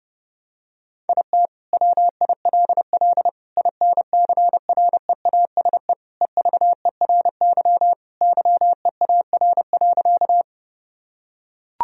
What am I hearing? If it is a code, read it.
Morse code, heard as a keyed tone, about 30 wpm, 700 Hz. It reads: IT WILL INCREASE EVERY YEAR.
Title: Sentences from Top 1000 Words - Review Part 4 - 30wpm